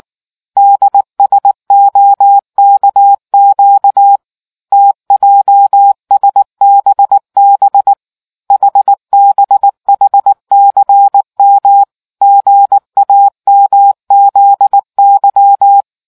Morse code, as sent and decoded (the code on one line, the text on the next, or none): -.. ... --- -.- --.- / - .--- ... -... -... / .... -... .... -.-. -- / --. .- -- --.. -.--
DSOKQ TJSBB HBHCM GAMZY